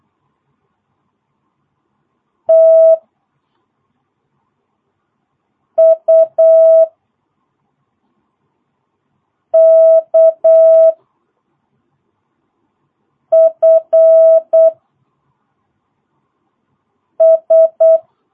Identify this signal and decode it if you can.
Morse code, heard as a keyed tone, about 8 wpm, 650 Hz. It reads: T U K F S
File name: jh8cbh / 2022cw